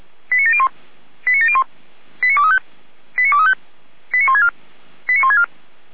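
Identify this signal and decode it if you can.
Five-tone selective call (ZVEI), heard as quick runs of five tones, tones of about 70 ms. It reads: 89701 89701 89235 89235 89164 89164
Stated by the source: Received on an SDR, ZVEI five-tone call.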